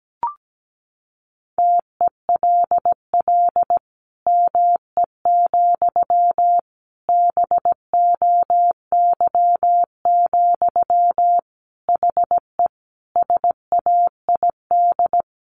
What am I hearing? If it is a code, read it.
Morse code, heard as a keyed tone, about 17 wpm, 700 Hz. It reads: TELL ME, BOY, HE SAID
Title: Sentences from Top 1500 Words 17wpm